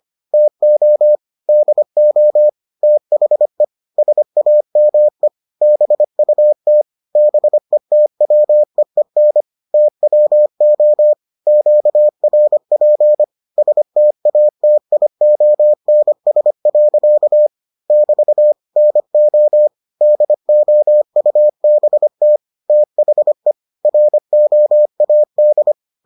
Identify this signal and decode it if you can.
Morse code, heard as a keyed tone, about 25 wpm, 600 Hz. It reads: TO DO THE SAME BUT BETWEEN TWO QRP STATIONS. <BT> NO DOUBT THE ROAD